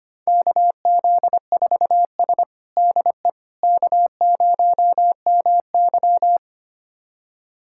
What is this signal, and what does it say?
Morse code, heard as a keyed tone, about 25 wpm, 700 Hz. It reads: KZ4S DE K0MY
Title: USA Callsign DE Callsign - 4 Characters - 25wpm